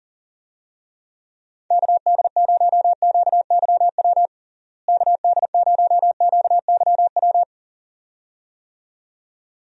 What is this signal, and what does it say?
Morse code, heard as a keyed tone, about 40 wpm, 700 Hz. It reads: KD0QYW KD0QYW
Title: USA Call Signs - 6 Characters - Repeated 2x - 40wpm